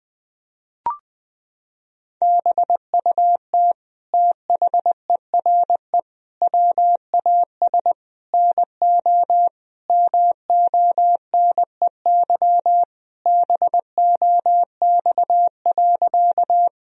Morse code, heard as a keyed tone, about 20 wpm, 700 Hz. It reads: BUT THERE WAS NO MONEY BOX.
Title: Sentences from Top 1000 Words - Review Part 2 - 20wpm